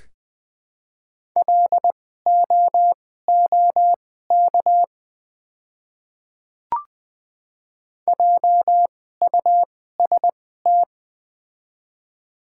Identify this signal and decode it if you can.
Morse code, heard as a keyed tone, about 20 wpm, spaced as for 15 wpm, 700 Hz. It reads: LOOK JUST
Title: 2x Character Spacing - Top 300 Words - 20wpm